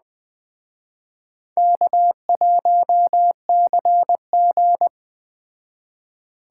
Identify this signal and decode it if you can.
Morse code, heard as a keyed tone, about 20 wpm, 700 Hz. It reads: K1CG